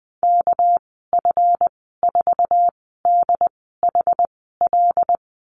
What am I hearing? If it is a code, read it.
Morse code, heard as a keyed tone, about 20 wpm, spaced as for 15 wpm, 700 Hz. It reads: KF4DHL
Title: 231 Lesson 37g - Letter x - Call Signs 20wpm